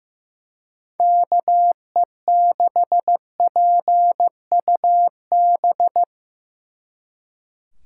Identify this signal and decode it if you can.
Morse code, heard as a keyed tone, about 17 wpm, 700 Hz. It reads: KE6PUB
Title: Speed Racing - USA Call Signs 15wpm